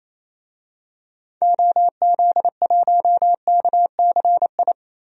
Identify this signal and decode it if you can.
Morse code, heard as a keyed tone, about 28 wpm, 700 Hz. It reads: OZ1KCI
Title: DXCC-100 Call Signs - 6 Characters - 28wpm